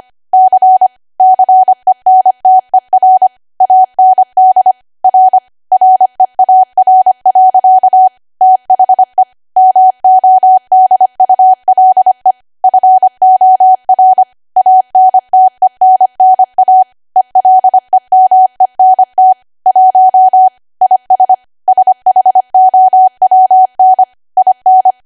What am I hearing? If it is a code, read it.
Morse code, heard as a keyed tone, about 25 wpm, 750 Hz. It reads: C CENTER AND R REAR. THE MODULE FOR ANTENNA ELEMENT 1 IS SHOWN IN
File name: cwa / W1AW-Level3 / 25WPM